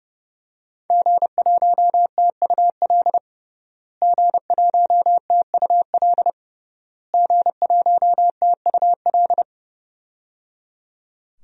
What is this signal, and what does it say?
Morse code, heard as a keyed tone, about 28 wpm, 700 Hz. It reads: G1TUL G1TUL G1TUL
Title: DXCC-100 Call Signs - 5 Characters - Repeated 3x - 30wpm